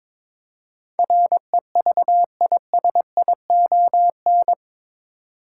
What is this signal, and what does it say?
Morse code, heard as a keyed tone, about 22 wpm, 700 Hz. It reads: REVISION
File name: practice / 22wpm / Eight Letter Words - 22wpm